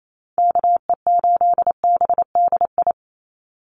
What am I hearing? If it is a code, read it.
Morse code, heard as a keyed tone, about 28 wpm, 700 Hz. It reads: KE8BDI
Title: USA Call Signs 28wpm